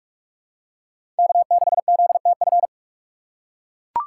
Morse code, heard as a keyed tone, about 42 wpm, 690 Hz. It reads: K6ZTF
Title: USA Call Signs - 5 Characters - 6x Character Spacing - 45wpm